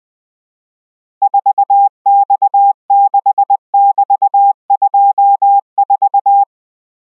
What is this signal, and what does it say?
Morse code, heard as a keyed tone, about 20 wpm, 800 Hz. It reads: 4X6<BT>24